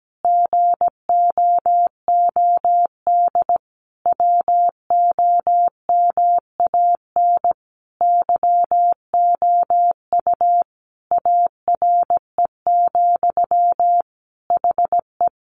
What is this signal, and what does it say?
Morse code, heard as a keyed tone, about 17 wpm, 700 Hz. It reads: GOOD WOMAN YOU ARE, HE
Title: Sentences from Top 1100 Words 17wpm